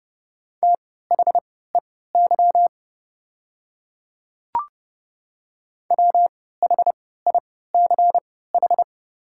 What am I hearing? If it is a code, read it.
Morse code, heard as a keyed tone, about 30 wpm, spaced as for 17 wpm, 700 Hz. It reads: THEY WHICH
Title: 3x Character Spacing - Top 100 Words - 30wpm